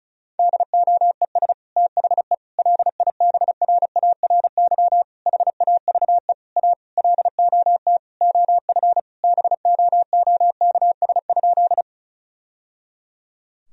Morse code, heard as a keyed tone, about 34 wpm, 700 Hz. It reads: DOES THE LIBRARY HAVE A LOT OF BOOKS?
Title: Sentences from Top 1000 Words - Review Part 4 - 35wpm